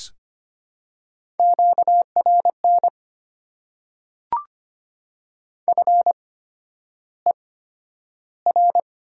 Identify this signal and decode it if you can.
Morse code, heard as a keyed tone, about 25 wpm, spaced as for 11 wpm, 700 Hz. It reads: QRN FER
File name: Top 90 QSO Elements - 8x Character Spacing - 25wpm